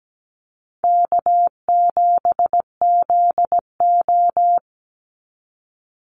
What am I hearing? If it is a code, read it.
Morse code, heard as a keyed tone, about 17 wpm, 700 Hz. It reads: K7ZO
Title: NA SOTA Chaser Callsigns 17wpm